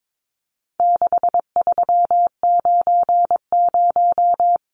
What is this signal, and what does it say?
Morse code, heard as a keyed tone, about 22 wpm, 700 Hz. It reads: 6390
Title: Numbers - 4-Digit 22wpm